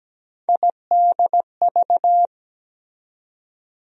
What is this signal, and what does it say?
Morse code, heard as a keyed tone, about 17 wpm, 700 Hz. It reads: IDV
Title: USA Call Sign Postfixes - 3 Letters 17wpm